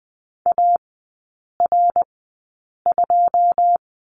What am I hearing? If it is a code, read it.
Morse code, heard as a keyed tone, about 20 wpm, spaced as for 8 wpm, 700 Hz. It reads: AR2
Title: 087 Lesson 15e - Number 5 - Three Characters 20wpm